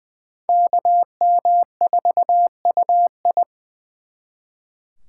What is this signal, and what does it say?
Morse code, heard as a keyed tone, about 24 wpm, 700 Hz. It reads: KM4UI
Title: Speed Racing - USA 2x2 Call Signs 20wpm